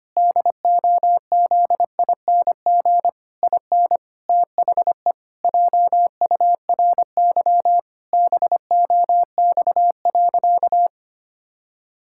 Morse code, heard as a keyed tone, about 25 wpm, 700 Hz. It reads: DOZING IN THE JURY BOX.
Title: Pangram Challenge v2 - 25wpm